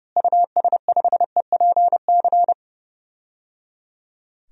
Morse code, heard as a keyed tone, about 30 wpm, 700 Hz. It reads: US5EPC
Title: DXCC-100 Call Signs - 6 Characters - Repeated 3x - 30wpm